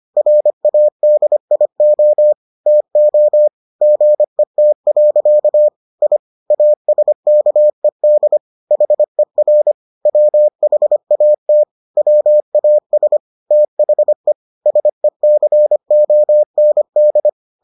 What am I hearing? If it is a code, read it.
Morse code, heard as a keyed tone, about 25 wpm, 600 Hz. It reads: RADIO TO GET. I ASKED HER WHAT WAS THE SECOND